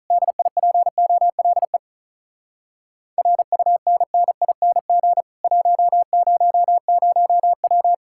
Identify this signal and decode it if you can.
Morse code, heard as a keyed tone, about 48 wpm, 700 Hz. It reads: DIPOLE RUNNING 100W